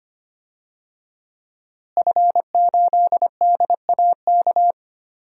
Morse code, heard as a keyed tone, about 25 wpm, 700 Hz. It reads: F8DAK
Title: DXCC-100 Call Signs - 5 Characters - Repeated 3x - 25wpm